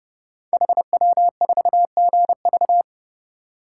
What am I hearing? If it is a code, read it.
Morse code, heard as a keyed tone, about 28 wpm, 700 Hz. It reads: HW4GV
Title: Code Groups - 5 Letter:Numbers - 30wpm